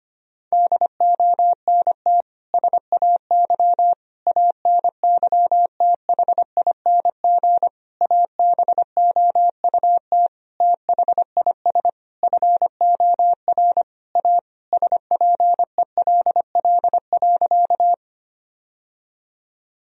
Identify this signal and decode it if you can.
Morse code, heard as a keyed tone, about 25 wpm, 700 Hz. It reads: DONT SAY ANYTHING ABOUT THIS FOR A SPELL.
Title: Sentences from Top 1000 Words - Review Part 1 - 25wpm